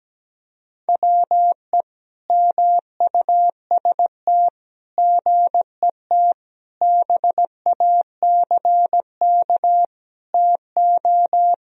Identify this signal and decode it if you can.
Morse code, heard as a keyed tone, about 17 wpm, 700 Hz. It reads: WE MUST GET BACK TO
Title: Sentences from Top 1000 Words - Review Part 2 - 17wpm